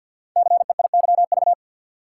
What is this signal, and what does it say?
Morse code, heard as a keyed tone, about 46 wpm, 700 Hz. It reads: CEIYV
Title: Code Groups - 5 Letter:Numbers - 50wpm